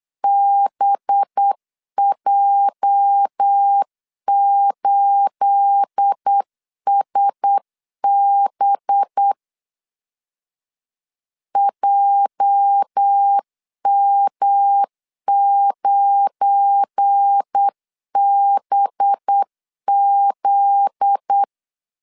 Morse code, heard as a keyed tone, about 8 wpm, 800 Hz. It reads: BJ8SB JM9BZ
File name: LEZIONI / 9 28 8wpm 3512